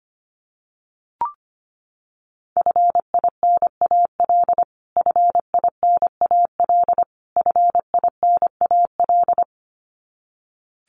Morse code, heard as a keyed tone, about 25 wpm, 700 Hz. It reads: FINAL FINAL FINAL
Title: Top 500 Words Repeated 25wpm